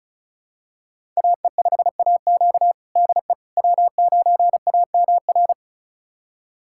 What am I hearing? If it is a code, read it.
Morse code, heard as a keyed tone, about 34 wpm, 700 Hz. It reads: AE5AQ DE W9AMR
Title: USA Callsign DE Callsign - 5 Characters - 35wpm